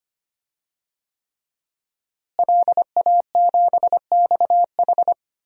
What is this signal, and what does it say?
Morse code, heard as a keyed tone, about 25 wpm, 700 Hz. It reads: LA7XH
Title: DXCC-100 Call Signs - 5 Characters - Repeated 3x - 25wpm